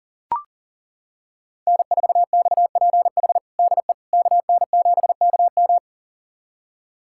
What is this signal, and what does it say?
Morse code, heard as a keyed tone, about 40 wpm, 700 Hz. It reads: N4XPH DE KN7KM
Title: USA Callsign DE Callsign - 5 Characters - 40wpm